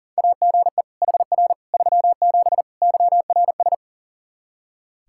Morse code, heard as a keyed tone, about 40 wpm, 700 Hz. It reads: AGE HR 37 YRS
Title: Practice QSO - Exchange 3 - More Info - 40wpm